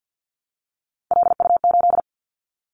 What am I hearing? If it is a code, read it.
Morse code, heard as a keyed tone, about 50 wpm, 700 Hz. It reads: LU8
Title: Callsign Prefixes for Top 50 DXCC - 50wpm